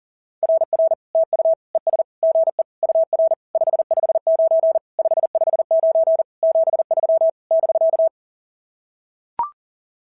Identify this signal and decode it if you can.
Morse code, heard as a keyed tone, about 40 wpm, 650 Hz. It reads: RR TU ES GE UR 559 559 73 <BK>
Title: SOTA QSO Practice 40wpm